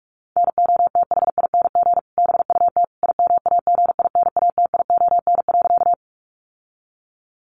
Morse code, heard as a keyed tone, about 44 wpm, 700 Hz. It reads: NOTHING BUT IMAGINATION.
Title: Sentences from Top 1300 Words 45wpm